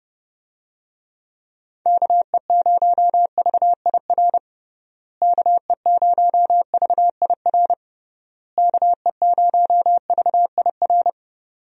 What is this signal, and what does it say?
Morse code, heard as a keyed tone, about 30 wpm, 700 Hz. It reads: KE0VIR KE0VIR KE0VIR
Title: USA Call Signs - 6 Characters - Repeated 3x - 30wpm